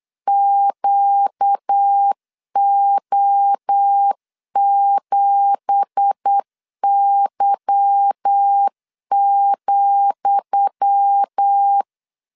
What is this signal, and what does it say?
Morse code, heard as a keyed tone, about 8 wpm, 800 Hz. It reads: QO7Y,
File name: LEZIONI / 10 SIGNS / 10D COMA 8wpm 3749